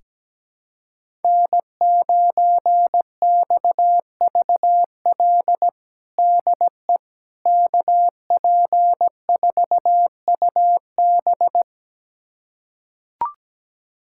Morse code, heard as a keyed tone, about 17 wpm, 700 Hz. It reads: N9XVL DE KP4UB
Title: USA Callsign DE Callsign 17wpm